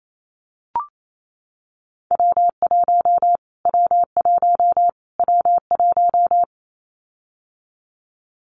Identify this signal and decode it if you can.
Morse code, heard as a keyed tone, about 28 wpm, 700 Hz. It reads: W1 W1 W1
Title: Callsign Prefixes for Top 25 DXCC - Repeated 3x - 28wpm